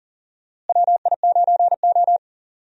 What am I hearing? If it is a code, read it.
Morse code, heard as a keyed tone, about 40 wpm, 700 Hz. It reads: WI9O
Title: USA Call Signs - 4 Characters - Repeated 2x - 40wpm